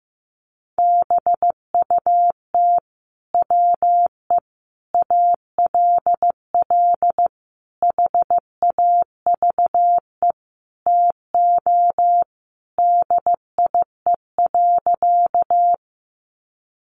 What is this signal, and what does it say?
Morse code, heard as a keyed tone, about 14 wpm, 700 Hz. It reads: BUT WE ALL HAVE TO DIE.
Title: Sentences from Top 1300 Words 15wpm